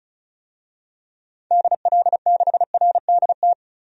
Morse code, heard as a keyed tone, about 36 wpm, 700 Hz. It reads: DL6RDT
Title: DXCC-100 Call Signs - 6 Characters - 35wpm